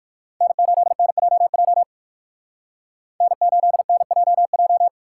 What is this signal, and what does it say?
Morse code, heard as a keyed tone, about 59 wpm, 700 Hz. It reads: N8NJJ N8NJJ